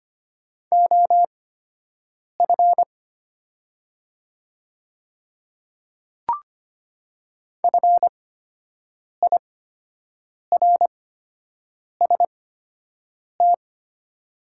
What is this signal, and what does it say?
Morse code, heard as a keyed tone, about 25 wpm, spaced as for 7 wpm, 700 Hz. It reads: OF FIRST
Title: 8x Character Spacing - Top 100 Words - 25wpm